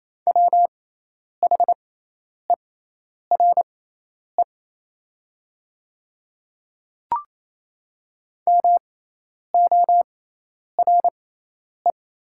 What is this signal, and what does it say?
Morse code, heard as a keyed tone, about 28 wpm, spaced as for 10 wpm, 700 Hz. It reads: WHERE MORE
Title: 6x Character Spacing - Top 300 Words - 28wpm